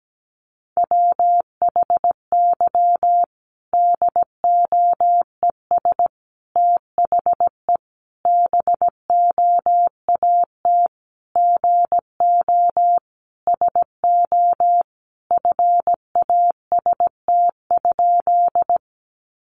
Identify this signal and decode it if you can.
Morse code, heard as a keyed tone, about 17 wpm, 700 Hz. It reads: WHY DOES THE BOAT GO SO FAST?
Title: Sentences from Top 500 Words 17wpm